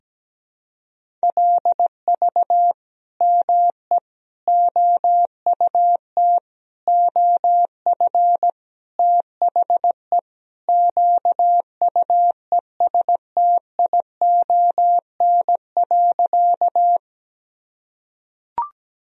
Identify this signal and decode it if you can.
Morse code, heard as a keyed tone, about 17 wpm, 700 Hz. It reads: LV ME OUT OF THE QUESTION.
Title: Sentences from Top 400 Words and Abbreviations 17wpm